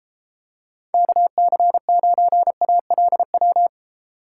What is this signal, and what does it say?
Morse code, heard as a keyed tone, about 28 wpm, 700 Hz. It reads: KC9ALW